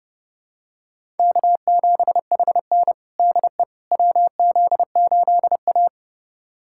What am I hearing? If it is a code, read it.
Morse code, heard as a keyed tone, about 32 wpm, 700 Hz. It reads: K7HN DE WZ8A
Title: USA Callsign DE Callsign - 4 Characters - 30wpm